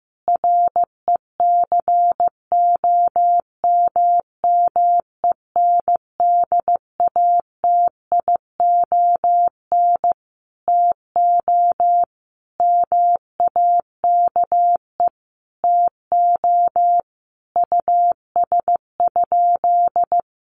Morse code, heard as a keyed tone, about 15 wpm, 700 Hz. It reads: RECOMMENDATION TO MAKE TO US?
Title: Sentences from Top 1400 Words 15wpm